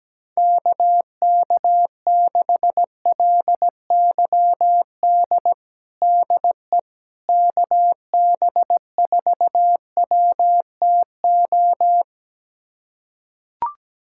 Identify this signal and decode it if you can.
Morse code, heard as a keyed tone, about 17 wpm, 700 Hz. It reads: KK6LYD DE KB4WTO
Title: USA Callsign DE Callsign - 6 Characters - 17wpm